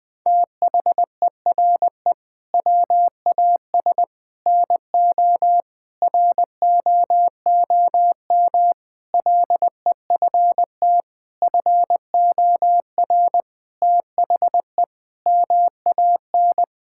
Morse code, heard as a keyed tone, about 20 wpm, 700 Hz. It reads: THERE WAS NO ROOM LEFT FOR THE MAN
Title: Sentences from Top 1600 Words 20wpm